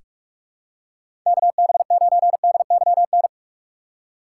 Morse code, heard as a keyed tone, about 46 wpm, 700 Hz. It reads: KB9DYN
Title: USA Call Signs - 3x Character Spacing - 45wpm